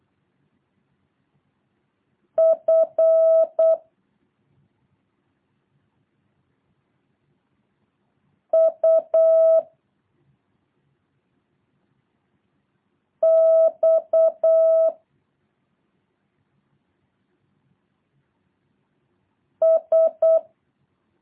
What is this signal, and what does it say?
Morse code, heard as a keyed tone, about 8 wpm, 650 Hz. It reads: F U X S
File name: jh8cbh / 2023cw